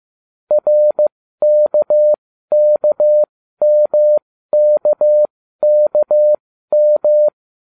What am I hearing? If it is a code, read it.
Morse code, heard as a keyed tone, about 15 wpm, spaced as for 12 wpm, 600 Hz. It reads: RKKMKKM